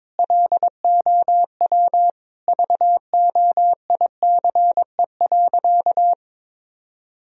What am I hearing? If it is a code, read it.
Morse code, heard as a keyed tone, about 22 wpm, 700 Hz. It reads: LOW VOICE.
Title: Sentences from Top 1000 Words - Review Part 1 - 22wpm